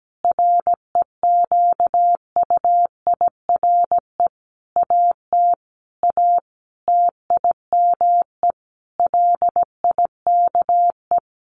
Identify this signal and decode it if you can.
Morse code, heard as a keyed tone, about 17 wpm, 700 Hz. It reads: REQUIRE AT A TIME LIKE
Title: Sentences from Top 900 Words 17wpm